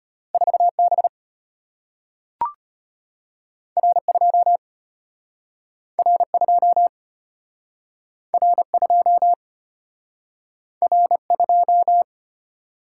Morse code, heard as a keyed tone, about 35 wpm, 700 Hz. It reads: 4B R2 R2 R2 R2